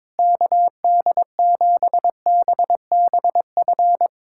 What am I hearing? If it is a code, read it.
Morse code, heard as a keyed tone, about 22 wpm, 700 Hz. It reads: KD7BBF
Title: USA Callsign DE Callsign - 6 Characters - 22wpm